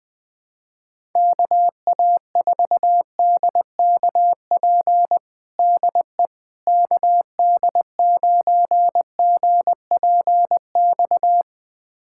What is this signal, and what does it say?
Morse code, heard as a keyed tone, about 20 wpm, 700 Hz. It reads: KA4DKP DE KD9GPX
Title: USA Callsign DE Callsign 20wpm